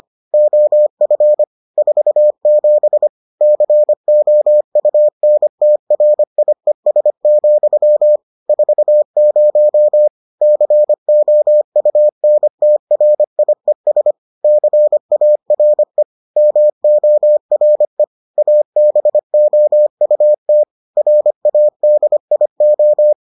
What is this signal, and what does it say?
Morse code, heard as a keyed tone, about 25 wpm, 600 Hz. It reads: OF 47 COUNTRIES, 40 COUNTRIES CARE MORE ABOUT RADIO